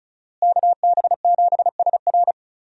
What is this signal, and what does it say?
Morse code, heard as a keyed tone, about 34 wpm, 700 Hz. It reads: KB7SR
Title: USA Call Signs - 5 Characters - Repeated 3x - 35wpm